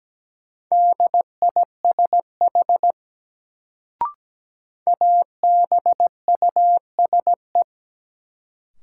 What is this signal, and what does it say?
Morse code, heard as a keyed tone, about 17 wpm, 700 Hz. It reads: DISH ABUSE
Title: Sentences from Top 1300 Words - New Word Drill Alt - 17wpm